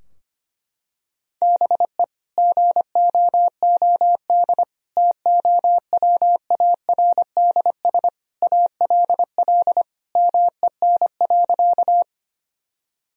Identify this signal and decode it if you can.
Morse code, heard as a keyed tone, about 25 wpm, 700 Hz. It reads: BE GOOD TOWARDS ALL MEN.
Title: Sentences from Top 1000 Words - Review Part 2 - 25wpm